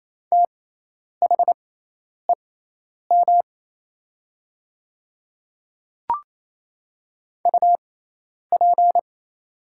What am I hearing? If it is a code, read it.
Morse code, heard as a keyed tone, about 28 wpm, spaced as for 10 wpm, 700 Hz. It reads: THEM UP
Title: 6x Character Spacing - Top 100 Words - 28wpm